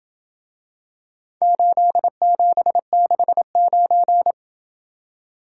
Morse code, heard as a keyed tone, about 27 wpm, 700 Hz. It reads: 8769